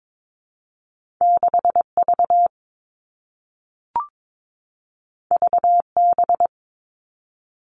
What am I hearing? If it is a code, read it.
Morse code, heard as a keyed tone, about 22 wpm, 700 Hz. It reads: 6 V V B
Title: Two Challenging Letter-Number Combinations 22wpm